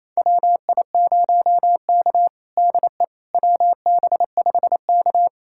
Morse code, heard as a keyed tone, about 28 wpm, 700 Hz. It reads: WI0K DE WB5K
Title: USA Callsign DE Callsign - 4 Characters - 28wpm